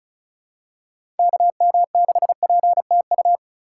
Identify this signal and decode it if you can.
Morse code, heard as a keyed tone, about 36 wpm, 700 Hz. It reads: KM6PTU
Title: USA Call Signs - 2x Character Spacing - 35wpm